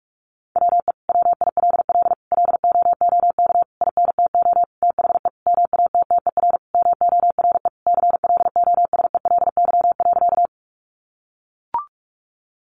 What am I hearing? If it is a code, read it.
Morse code, heard as a keyed tone, about 44 wpm, 700 Hz. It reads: WE WILL LOOK INTO THE MATTER MORE CLOSELY.
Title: Sentences from Top 1400 Words 45wpm